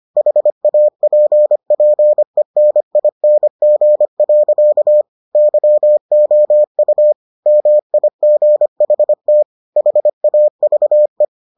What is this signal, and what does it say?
Morse code, heard as a keyed tone, about 25 wpm, 600 Hz. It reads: HAPPENING. YOU MIGHT HAVE